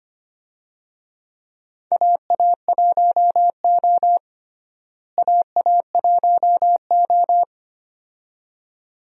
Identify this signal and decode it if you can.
Morse code, heard as a keyed tone, about 25 wpm, 700 Hz. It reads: AA1O AA1O
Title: USA Call Signs - 4 Characters - Repeated 2x - 25wpm